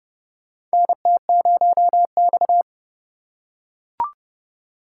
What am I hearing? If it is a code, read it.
Morse code, heard as a keyed tone, about 28 wpm, 700 Hz. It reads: NT0X
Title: USA Call Signs - 4 Characters - 3x Character Spacing - 30wpm